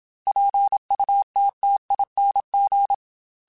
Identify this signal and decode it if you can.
Morse code, heard as a keyed tone, about 27 wpm, 800 Hz. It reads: PUTTING